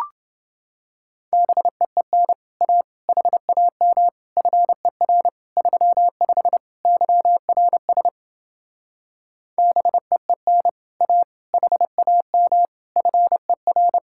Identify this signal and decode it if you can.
Morse code, heard as a keyed tone, about 29 wpm, 700 Hz. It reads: BEEN A HAM FER 35 YRS BEEN A HAM FER